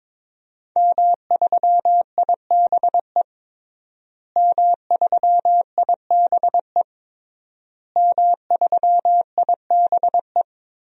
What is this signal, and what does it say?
Morse code, heard as a keyed tone, about 22 wpm, 700 Hz. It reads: M3IBE M3IBE M3IBE
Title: DXCC-100 Call Signs - 5 Characters - Repeated 3x - 22wpm